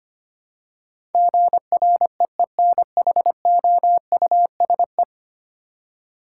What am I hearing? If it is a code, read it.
Morse code, heard as a keyed tone, about 25 wpm, 690 Hz. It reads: GREENHOUSE
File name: 10 Letter Words - First 3 Letters Given - 25wpm